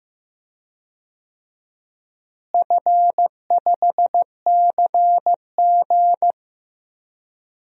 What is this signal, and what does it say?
Morse code, heard as a keyed tone, about 15 wpm, 700 Hz. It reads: F5CG
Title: DXCC-100 Call Signs - 4 Characters - Repeated 3x - 15wpm